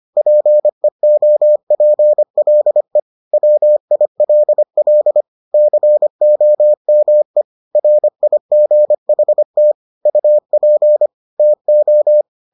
Morse code, heard as a keyed tone, about 25 wpm, 600 Hz. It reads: PEOPLE WILL COME RIGHT UP TO